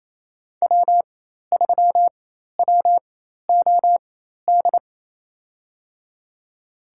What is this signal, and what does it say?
Morse code, heard as a keyed tone, about 28 wpm, spaced as for 13 wpm, 700 Hz. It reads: W3WOD
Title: USA Call Signs - 5 Characters - 4x Character Spacing - 28wpm